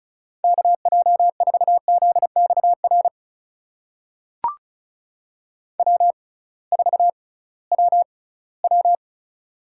Morse code, heard as a keyed tone, about 36 wpm, spaced as for 20 wpm, 700 Hz. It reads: KJ4ZXR W4WW
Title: USA Call Signs - 6x Character Spacing - 35wpm